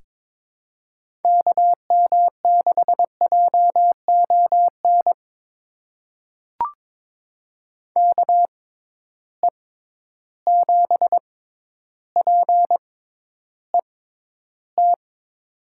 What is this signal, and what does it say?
Morse code, heard as a keyed tone, about 22 wpm, spaced as for 11 wpm, 700 Hz. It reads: KM6JON KE7PET
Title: USA Call Signs - 6 Characters - 6x Character Spacing - 22wpm